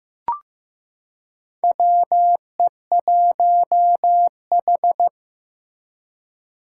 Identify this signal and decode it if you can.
Morse code, heard as a keyed tone, about 15 wpm, 700 Hz. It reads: WE1H
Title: Four Challenging Letter-Number Combinations 15wpm